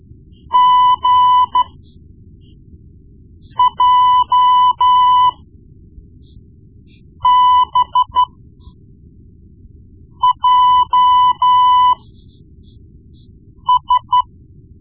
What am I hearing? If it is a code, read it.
Morse code, heard as a keyed tone, about 13 wpm, spaced as for 4 wpm, 1000 Hz. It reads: GJBJS
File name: Morse Lessons / Morse Lessons / Tape-1B